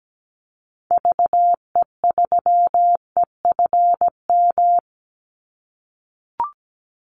Morse code, heard as a keyed tone, about 17 wpm, 700 Hz. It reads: VE3EFM
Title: Call Signs - 17wpm